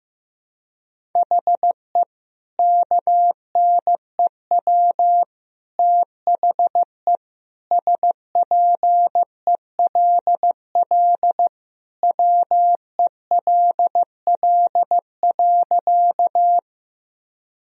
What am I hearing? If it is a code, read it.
Morse code, heard as a keyed tone, about 15 wpm, 700 Hz. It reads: HE KNEW THE SPELL WELL.
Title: Sentences from Top 1000 Words - Review Part 1 - 15wpm